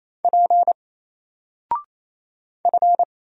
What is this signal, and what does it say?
Morse code, heard as a keyed tone, about 28 wpm, 700 Hz. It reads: P F
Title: Single Character (Letter:Number:Punctuation:Prosign) Alt 28wpm